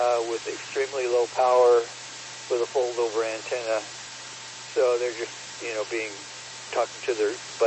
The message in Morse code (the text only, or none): none